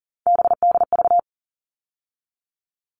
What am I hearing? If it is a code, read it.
Morse code, heard as a keyed tone, about 40 wpm, 700 Hz. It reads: BDV
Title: All Cut Numbers - 3-Digit 40wpm